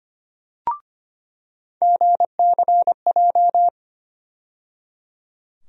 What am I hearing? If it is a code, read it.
Morse code, heard as a keyed tone, about 25 wpm, 700 Hz. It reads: GCJ
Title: USA Call Sign Postfixes - 3 Letters 25wpm